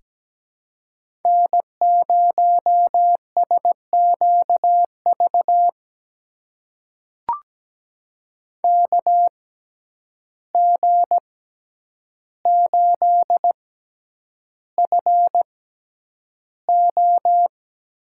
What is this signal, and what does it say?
Morse code, heard as a keyed tone, about 17 wpm, spaced as for 9 wpm, 700 Hz. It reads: N0SQV KG8FO
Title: USA Call Signs - 5 Characters - 6x Character Spacing - 17wpm